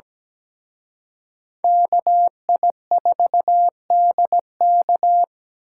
Morse code, heard as a keyed tone, about 20 wpm, 700 Hz. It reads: KI4DK
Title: Speed Racing - USA 2x2 Call Signs 17wpm